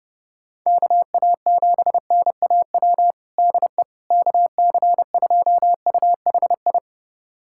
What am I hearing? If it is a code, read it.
Morse code, heard as a keyed tone, about 32 wpm, 700 Hz. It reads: KA7NAW DE KC2UHI
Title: USA Callsign DE Callsign 30wpm